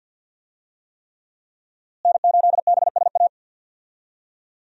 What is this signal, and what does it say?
Morse code, heard as a keyed tone, about 52 wpm, 700 Hz. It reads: N8BSA
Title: USA Call Signs - 5 Characters - Repeated 3x - 50wpm